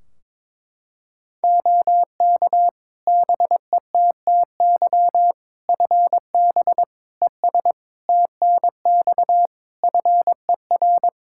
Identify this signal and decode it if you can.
Morse code, heard as a keyed tone, about 22 wpm, 700 Hz. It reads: OK BETTY FB ES TNX FER
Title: Practice QSO - Exchange 2 - Info - 22wpm